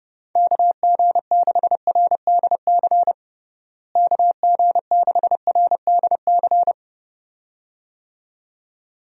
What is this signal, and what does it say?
Morse code, heard as a keyed tone, about 32 wpm, 700 Hz. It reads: KG6RDC KG6RDC
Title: USA Call Signs - 6 Characters - Repeated 2x - 30wpm